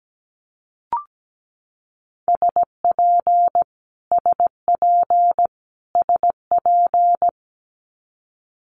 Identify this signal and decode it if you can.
Morse code, heard as a keyed tone, about 17 wpm, 700 Hz. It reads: SP SP SP
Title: Multi-letter Phonemes Repeated 3x 17wpm